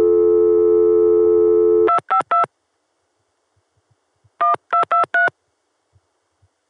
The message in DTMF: 252 1223